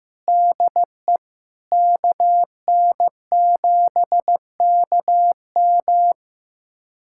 Morse code, heard as a keyed tone, about 15 wpm, 700 Hz. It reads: DE KN7KM
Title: USA Callsign DE Callsign - 5 Characters - 15wpm